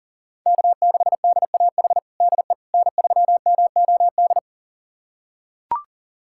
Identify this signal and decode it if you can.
Morse code, heard as a keyed tone, about 40 wpm, 700 Hz. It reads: K6DAH DE N3MOD
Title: USA Callsign DE Callsign - 5 Characters - 40wpm